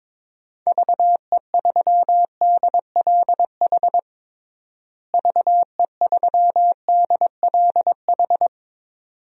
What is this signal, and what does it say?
Morse code, heard as a keyed tone, about 22 wpm, 700 Hz. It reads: VE3DLH VE3DLH
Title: DXCC-100 Call Signs - 6 Characters - Repeated 2x - 22wpm